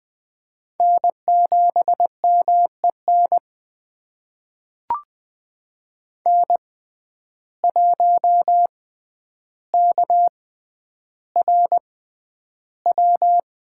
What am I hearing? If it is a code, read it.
Morse code, heard as a keyed tone, about 21 wpm, spaced as for 10 wpm, 700 Hz. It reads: N7MEN N1KRW
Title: USA Call Signs - 5 Characters - 6x Character Spacing - 20wpm